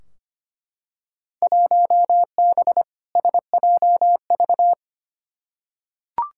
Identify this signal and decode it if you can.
Morse code, heard as a keyed tone, about 25 wpm, 700 Hz. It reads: 1B SJV
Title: Field Day Exchanges Encore - 25wpm